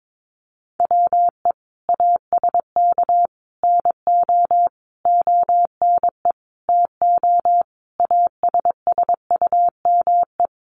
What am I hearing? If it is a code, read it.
Morse code, heard as a keyed tone, about 22 wpm, 700 Hz. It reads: WE ASK NO ONE TO ASSUME
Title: Sentences from Top 1500 Words 22wpm